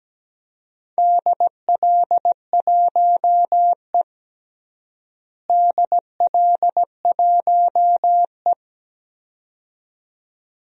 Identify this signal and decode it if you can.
Morse code, heard as a keyed tone, about 17 wpm, 700 Hz. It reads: DL1E DL1E
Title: DXCC-100 Call Signs - 4 Characters - Repeated 2x - 17wpm